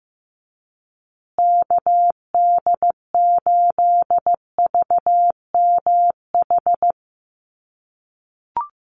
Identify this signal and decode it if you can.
Morse code, heard as a keyed tone, about 15 wpm, 700 Hz. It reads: KD8VMH
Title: USA Call Signs Repeated 3x 15wpm